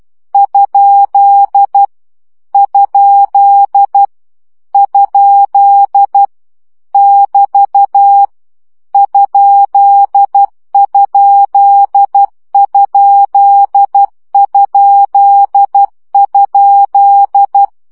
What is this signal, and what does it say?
Morse code, heard as a keyed tone, about 12 wpm, 800 Hz. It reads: ? ? ? <BT> ?????
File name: CW gyakorlas kerdojel 10 baud